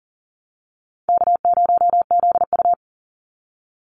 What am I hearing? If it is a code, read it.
Morse code, heard as a keyed tone, about 40 wpm, 700 Hz. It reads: K0ZU
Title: Easy Call Signs - 40wpm